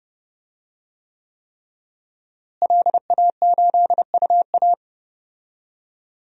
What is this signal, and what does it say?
Morse code, heard as a keyed tone, about 32 wpm, 700 Hz. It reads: LA8UA
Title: DXCC-100 Call Signs - 5 Characters - Repeated 3x - 30wpm